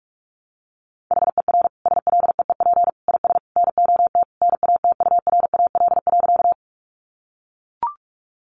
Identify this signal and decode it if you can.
Morse code, heard as a keyed tone, about 44 wpm, 700 Hz. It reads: HER SLEEP IS NOT NATURAL.
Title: Sentences from Top 700 Words 45wpm